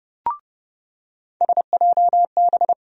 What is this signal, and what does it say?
Morse code, heard as a keyed tone, about 30 wpm, 700 Hz. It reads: SJB
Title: USA Call Sign Postfixes - 3 Letters 30wpm